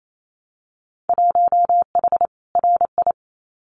Morse code, heard as a keyed tone, about 28 wpm, 700 Hz. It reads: 1H RI
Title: Winter Field Day Exchanges 28wpm